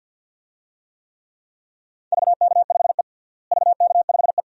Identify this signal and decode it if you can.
Morse code, heard as a keyed tone, about 53 wpm, 700 Hz. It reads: VK5E VK5E
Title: DXCC-100 Call Signs - 4 Characters - Repeated 2x - 50wpm